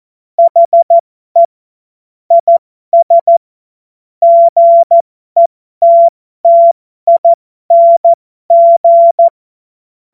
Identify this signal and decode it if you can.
Morse code, heard as a keyed tone, about 14 wpm, spaced as for 12 wpm, 680 Hz. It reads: HE IS GETTING
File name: Morsekurs / Witze